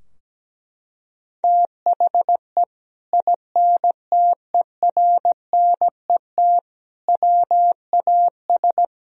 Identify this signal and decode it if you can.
Morse code, heard as a keyed tone, about 17 wpm, 700 Hz. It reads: THE INTERNET WAS
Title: Sentences from Top 1000 Words - Review Part 4 - 17wpm